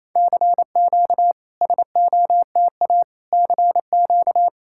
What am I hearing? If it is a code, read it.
Morse code, heard as a keyed tone, about 28 wpm, 690 Hz. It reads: CQ SOTA CQ
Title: SOTA QSO Practice 28wpm